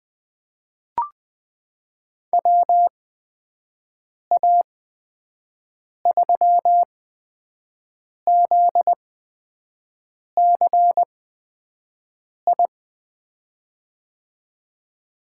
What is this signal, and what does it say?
Morse code, heard as a keyed tone, about 20 wpm, spaced as for 5 wpm, 700 Hz. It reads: WA3ZCI
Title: USA Call Signs - 6 Characters - 8x Character Spacing - 20wpm